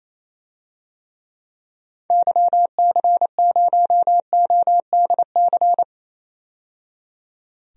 Morse code, heard as a keyed tone, about 27 wpm, 700 Hz. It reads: YC0ODC
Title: DXCC-100 Call Signs - 6 Characters - 28wpm